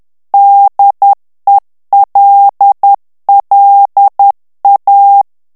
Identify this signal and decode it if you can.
Morse code, heard as a keyed tone, about 10 wpm, 800 Hz. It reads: DELLA
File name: morse mp3 / ITALIEN texte clair 1 vitesse 10 mots minute